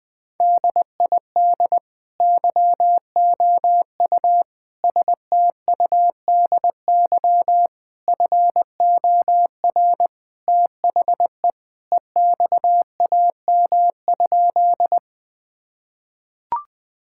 Morse code, heard as a keyed tone, about 20 wpm, 700 Hz. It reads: DID YOU STUDY FOR THE EXAM?
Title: Sentences from Top 1000 Words - Review Part 5 - 20wpm